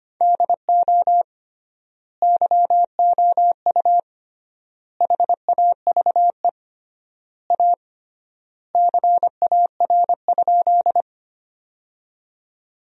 Morse code, heard as a keyed tone, about 25 wpm, 700 Hz. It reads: DO YOU HAVE A CAR?
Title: 3x Word-Spacing - Sentences from Top 300 Words - 25wpm